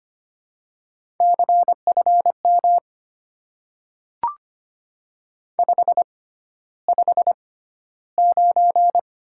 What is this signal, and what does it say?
Morse code, heard as a keyed tone, about 25 wpm, spaced as for 13 wpm, 700 Hz. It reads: CFM 559
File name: Top 90 QSO Elements - 6x Character Spacing - 25wpm